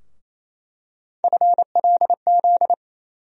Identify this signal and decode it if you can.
Morse code, heard as a keyed tone, about 28 wpm, 700 Hz. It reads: FLZ
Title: USA Call Sign Postfixes - 3 Letters 28wpm